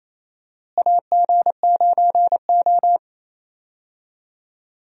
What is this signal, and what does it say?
Morse code, heard as a keyed tone, about 28 wpm, 700 Hz. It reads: AG9O
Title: USA Call Signs - 4 Characters - Repeated 2x - 28wpm